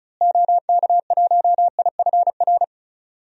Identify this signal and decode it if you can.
Morse code, heard as a keyed tone, about 36 wpm, 700 Hz. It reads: OK1IFR
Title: DXCC-100 Call Signs - 6 Characters - Repeated 3x - 35wpm